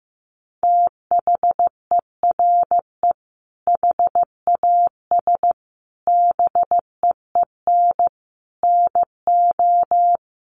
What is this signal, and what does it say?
Morse code, heard as a keyed tone, about 14 wpm, 700 Hz. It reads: THERE HAS BEEN NO
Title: Sentences from Top 700 Words 15wpm